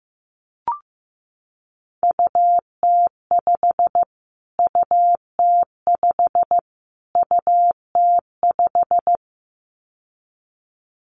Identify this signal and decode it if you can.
Morse code, heard as a keyed tone, about 14 wpm, 700 Hz. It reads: UT5 UT5 UT5
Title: Callsign Prefixes for Top 25 DXCC - Repeated 3x - 15wpm